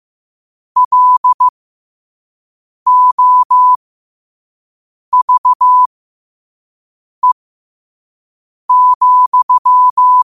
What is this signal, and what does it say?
Morse code, heard as a keyed tone, about 15 wpm, spaced as for 5 wpm, 1000 Hz. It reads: LOVE,
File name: qotdfiles / 5wpm